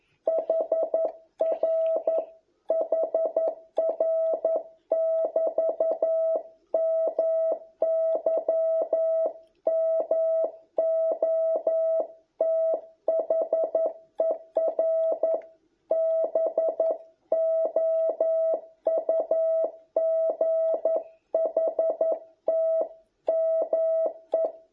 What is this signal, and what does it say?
Morse code, heard as a keyed tone, about 11 wpm, 660 Hz. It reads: HR HR <BT> MY MOTHER BOUGHT ME